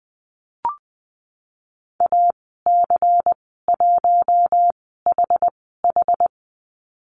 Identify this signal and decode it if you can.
Morse code, heard as a keyed tone, about 20 wpm, spaced as for 14 wpm, 700 Hz. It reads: AC1HH
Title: 161 Lesson 26g - Letter g - Call Signs 20wpm